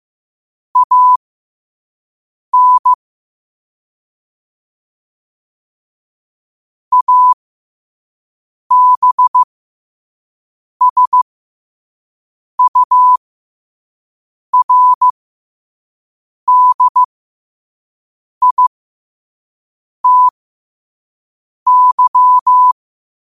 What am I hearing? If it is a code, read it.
Morse code, heard as a keyed tone, about 15 wpm, spaced as for 5 wpm, 1000 Hz. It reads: AN ABSURDITY
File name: qotdfiles / 5wpm